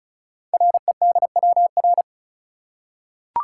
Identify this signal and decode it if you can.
Morse code, heard as a keyed tone, about 36 wpm, 700 Hz. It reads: REDWR
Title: Code Groups - 5 Letters - 35wpm